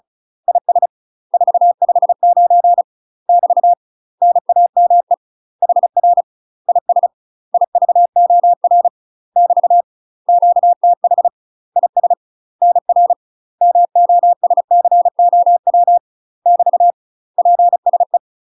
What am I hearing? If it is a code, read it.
Morse code, heard as a keyed tone, about 34 wpm, 700 Hz. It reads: IS 459 <BT> NAME HR IS IVOR <BT> QTH IS NR MOSCOW <BT> PSE
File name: UK MASTER QSOs TRAINING / 35WPM / UK MASTER QSOs 0024